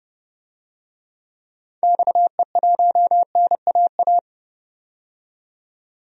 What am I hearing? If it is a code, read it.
Morse code, heard as a keyed tone, about 30 wpm, 700 Hz. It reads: XE1NAA
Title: DXCC-100 Call Signs - 6 Characters - 30wpm